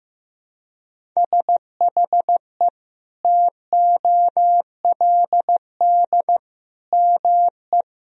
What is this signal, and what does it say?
Morse code, heard as a keyed tone, about 15 wpm, 700 Hz. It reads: SHE TOLD ME
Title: Sentences from Top 400 Words and Abbreviations 15wpm